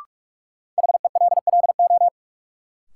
Morse code, heard as a keyed tone, about 46 wpm, 700 Hz. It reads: HELLO
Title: Sentences from Top 1100 Words - New Word Drill Alt - 45wpm